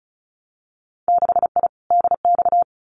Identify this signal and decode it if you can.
Morse code, heard as a keyed tone, about 36 wpm, 700 Hz. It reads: 6I DX
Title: Winter Field Day Exchanges Encore 35wpm